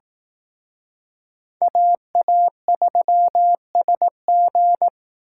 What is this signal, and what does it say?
Morse code, heard as a keyed tone, about 19 wpm, 700 Hz. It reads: AA3SG
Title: Speed Racing - USA 2x2 Call Signs 15wpm